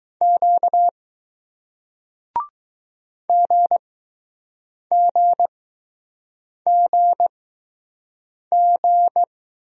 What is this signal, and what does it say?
Morse code, heard as a keyed tone, about 20 wpm, spaced as for 6 wpm, 700 Hz. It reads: Q GGGG